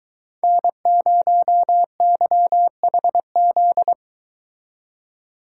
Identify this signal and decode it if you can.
Morse code, heard as a keyed tone, about 23 wpm, 700 Hz. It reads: N0YHZ